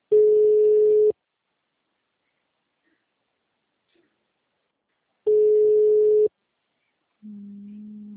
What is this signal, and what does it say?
Ringback tone (Europe, 425 Hz).
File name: out- -2017- - - .256